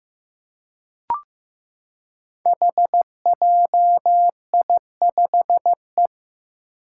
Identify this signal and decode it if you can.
Morse code, heard as a keyed tone, about 15 wpm, 700 Hz. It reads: HJI5E
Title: Five Challenging Letter-Number Combinations 15wpm